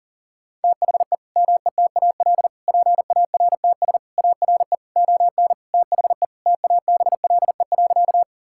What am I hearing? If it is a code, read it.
Morse code, heard as a keyed tone, about 40 wpm, 700 Hz. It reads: THE METAL PARTS ARE ON THE TABLE.
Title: Sentences from Top 1000 Words - Review Part 6 - 40wpm